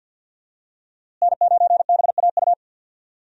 Morse code, heard as a keyed tone, about 51 wpm, 700 Hz. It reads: N9BAU
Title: USA Call Signs - 5 Characters - Repeated 3x - 50wpm